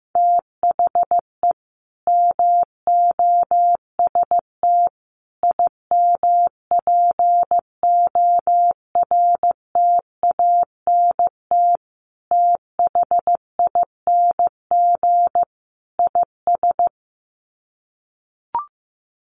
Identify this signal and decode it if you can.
Morse code, heard as a keyed tone, about 14 wpm, 700 Hz. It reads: THE MOST IMPORTANT THING IS
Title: Sets of 5 Words Encore 15wpm